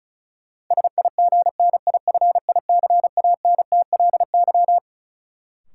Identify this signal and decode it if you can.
Morse code, heard as a keyed tone, about 35 wpm, 700 Hz. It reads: SIGNIFICANTLY
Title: Sentences from Top 1500 Words - New Word Drill Alt - 35wpm